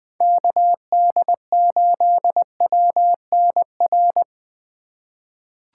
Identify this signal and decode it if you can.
Morse code, heard as a keyed tone, about 20 wpm, 700 Hz. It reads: KD8WNR
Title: USA Call Signs - 6 Characters - 4x Character Spacing - 20wpm